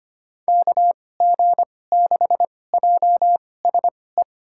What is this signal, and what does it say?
Morse code, heard as a keyed tone, about 25 wpm, spaced as for 18 wpm, 700 Hz. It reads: KG6JSE
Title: USA Call Signs - 6 Characters - 2x Character Spacing - 25wpm